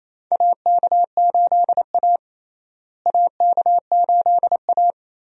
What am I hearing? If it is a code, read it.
Morse code, heard as a keyed tone, about 28 wpm, 700 Hz. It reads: AK8A AK8A
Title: USA Call Signs - 4 Characters - Repeated 3x - 28wpm